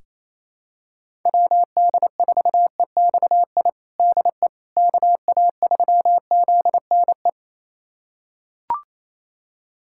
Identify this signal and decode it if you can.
Morse code, heard as a keyed tone, about 28 wpm, 700 Hz. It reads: WD4EXI DE KA3ZNE
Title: USA Callsign DE Callsign - 6 Characters - 28wpm